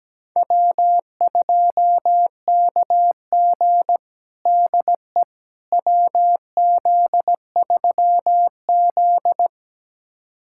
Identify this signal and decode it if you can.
Morse code, heard as a keyed tone, about 17 wpm, 700 Hz. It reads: W2KG DE WZ3Z
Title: USA Callsign DE Callsign - 4 Characters - 17wpm